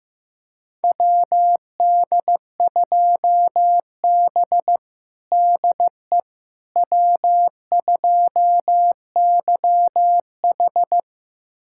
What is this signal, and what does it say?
Morse code, heard as a keyed tone, about 16 wpm, 700 Hz. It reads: WD2B DE W2YH
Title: USA Callsign DE Callsign - 4 Characters - 15wpm